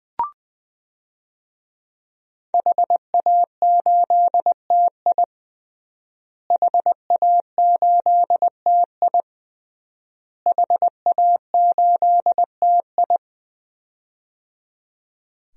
Morse code, heard as a keyed tone, about 20 wpm, 700 Hz. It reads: HA8TI HA8TI HA8TI
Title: DXCC-100 Call Signs - 5 Characters - Repeated 3x - 20wpm